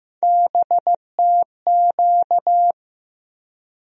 Morse code, heard as a keyed tone, about 15 wpm, 700 Hz. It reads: BTQ
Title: USA Call Sign Postfixes - 3 Letters 15wpm